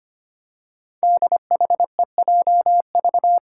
Morse code, heard as a keyed tone, about 25 wpm, 700 Hz. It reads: DHEJV
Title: Five Challenging Letter-Number Combinations 25wpm